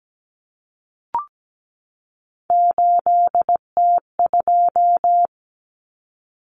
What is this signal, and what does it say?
Morse code, heard as a keyed tone, about 17 wpm, 700 Hz. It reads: 8T2
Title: Common Cut Numbers - 3-Digit 17wpm